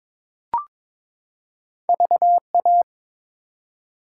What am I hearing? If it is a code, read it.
Morse code, heard as a keyed tone, about 22 wpm, 700 Hz. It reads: VA
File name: practice / 22wpm / ARRL Sections v2 - 22wpm